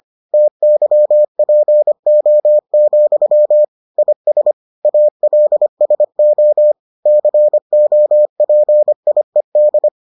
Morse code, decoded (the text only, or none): TYPO, IS ALSO COPIED